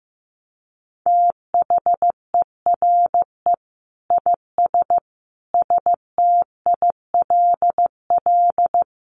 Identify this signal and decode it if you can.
Morse code, heard as a keyed tone, about 15 wpm, 700 Hz. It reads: THERE IS STILL
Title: Sentences from Top 1000 Words 15wpm